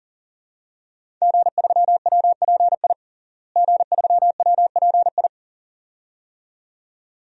Morse code, heard as a keyed tone, about 40 wpm, 700 Hz. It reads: G3WPI G3WPI
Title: DXCC-100 Call Signs - 5 Characters - Repeated 2x - 40wpm